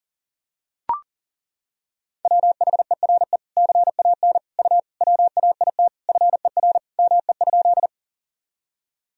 Morse code, heard as a keyed tone, about 40 wpm, 700 Hz. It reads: WHERE CAN U WAIT FER ME?
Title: Sentences from Top 500 Words and Abbreviations 40wpm